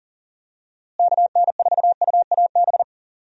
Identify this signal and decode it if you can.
Morse code, heard as a keyed tone, about 39 wpm, 700 Hz. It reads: KN4UAB
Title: USA Call Signs - 6 Characters - Repeated 2x - 40wpm